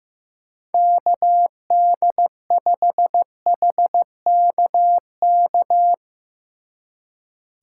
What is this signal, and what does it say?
Morse code, heard as a keyed tone, about 16 wpm, 700 Hz. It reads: KD5HKK
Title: Speed Racing - USA 2x3 Call Signs 15wpm